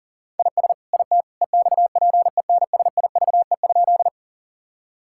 Morse code, heard as a keyed tone, about 40 wpm, 700 Hz. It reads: IS IT EXPENSIVE?
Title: Sentences from Top 1000 Words - Review Part 6 - 40wpm